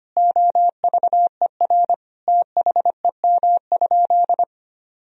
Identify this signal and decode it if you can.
Morse code, heard as a keyed tone, about 25 wpm, 700 Hz. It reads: OVER THEM?
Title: Sentences from Top 1000 Words - Review Part 3 - 25wpm